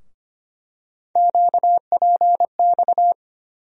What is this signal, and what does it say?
Morse code, heard as a keyed tone, about 25 wpm, 700 Hz. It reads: QPX
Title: USA Call Sign Postfixes - 3 Letters 25wpm